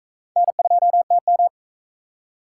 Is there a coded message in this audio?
Yes: Morse code from 0.4 to 1.5 s, heard as a keyed tone, about 44 wpm, 700 Hz. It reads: N2TM